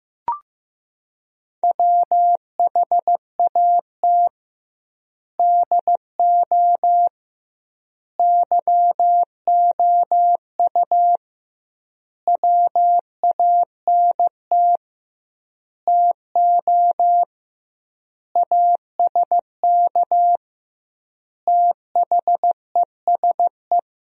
Morse code, heard as a keyed tone, about 15 wpm, 700 Hz. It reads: WHAT DO YOU WANT TO ASK THESE
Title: 2x Word-Spacing - Sentences from Top 200 Words - 15wpm